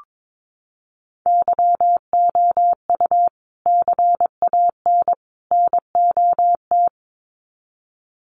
Morse code, heard as a keyed tone, about 22 wpm, 700 Hz. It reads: YOU CAN NOT
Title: Sets of 3 Words 22wpm